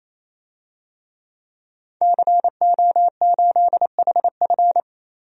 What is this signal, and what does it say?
Morse code, heard as a keyed tone, about 28 wpm, 700 Hz. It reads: CO8HF
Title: DXCC-100 Call Signs - 5 Characters - 28wpm